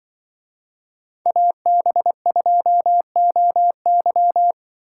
Morse code, heard as a keyed tone, about 25 wpm, 700 Hz. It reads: AB2OY